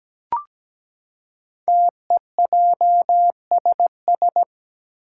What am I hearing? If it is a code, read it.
Morse code, heard as a keyed tone, about 17 wpm, 700 Hz. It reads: TEJSS
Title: Code Groups - 5 Letters - 17wpm